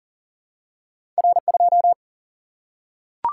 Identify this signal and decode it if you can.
Morse code, heard as a keyed tone, about 42 wpm, 700 Hz. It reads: R2
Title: Single-Letter Single-Number 40wpm